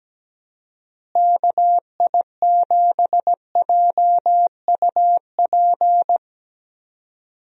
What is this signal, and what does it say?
Morse code, heard as a keyed tone, about 17 wpm, 700 Hz. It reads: KI7JUP
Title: USA Call Signs - 6 Characters - Repeated 2x - 17wpm